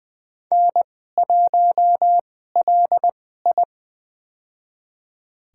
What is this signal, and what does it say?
Morse code, heard as a keyed tone, about 20 wpm, spaced as for 14 wpm, 700 Hz. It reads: N1LI
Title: USA Call Signs - 4 Characters - 2x Character Spacing - 20wpm